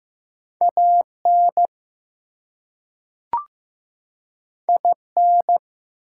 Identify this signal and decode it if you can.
Morse code, heard as a keyed tone, about 15 wpm, 700 Hz. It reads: AN IN
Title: Two Letter Words - 15wpm